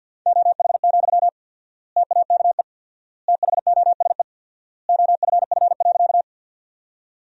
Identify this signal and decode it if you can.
Morse code, heard as a keyed tone, about 48 wpm, 700 Hz. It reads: OH, TAKE THOSE OFF.
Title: 4x Word-Spacing - Sentences from Top 500 Words - 50wpm